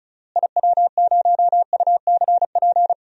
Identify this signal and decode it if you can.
Morse code, heard as a keyed tone, about 36 wpm, 700 Hz. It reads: IW0UCP
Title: DXCC-100 Call Signs - 6 Characters - Repeated 3x - 35wpm